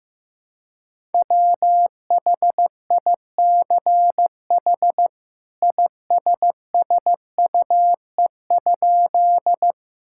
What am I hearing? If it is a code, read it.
Morse code, heard as a keyed tone, about 15 wpm, 700 Hz. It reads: WHICH ISSUE?
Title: Sentences from Top 1000 Words - Review Part 6 - 15wpm